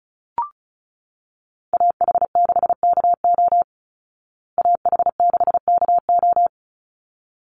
Morse code, heard as a keyed tone, about 34 wpm, 700 Hz. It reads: AH6KO AH6KO
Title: Call Signs Repeated 3x - 35wpm